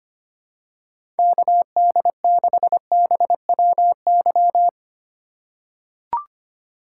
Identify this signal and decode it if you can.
Morse code, heard as a keyed tone, about 25 wpm, 690 Hz. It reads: KD6BWY
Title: USA Call Signs - 6 Characters - 3x Character Spacing - 25wpm